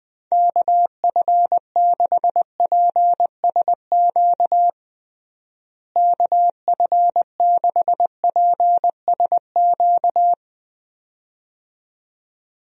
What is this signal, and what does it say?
Morse code, heard as a keyed tone, about 21 wpm, 700 Hz. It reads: KF6PSQ KF6PSQ
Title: USA Call Signs - 6 Characters - Repeated 2x - 20wpm